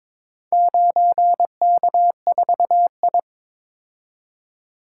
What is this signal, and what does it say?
Morse code, heard as a keyed tone, about 22 wpm, 700 Hz. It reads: 9K4I
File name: practice / 22wpm / Four Letter-Number Combinations - 22wpm